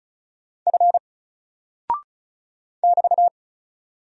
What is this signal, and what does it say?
Morse code, heard as a keyed tone, about 35 wpm, 700 Hz. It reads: F <BT>
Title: Single Character (Letter:Number:Punctuation:Prosign) 35wpm